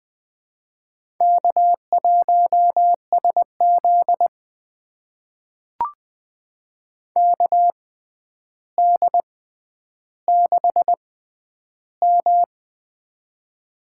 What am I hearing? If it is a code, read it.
Morse code, heard as a keyed tone, about 20 wpm, spaced as for 10 wpm, 700 Hz. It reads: K1SZ KD6M
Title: USA Call Signs - 4 Characters - 6x Character Spacing - 20wpm